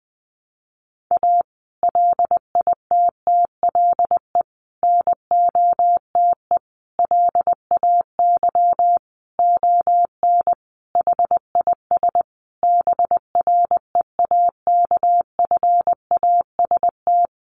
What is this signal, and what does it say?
Morse code, heard as a keyed tone, about 20 wpm, 700 Hz. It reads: A LITTLE NOTE LAY ON HIS BREAKFAST
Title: Sentences from Top 1500 Words 20wpm